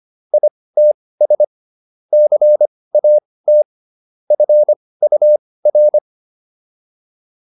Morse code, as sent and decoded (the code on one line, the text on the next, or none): .. - ... / -.-. .- - / ..-. ..- .-.
ITS CAT FUR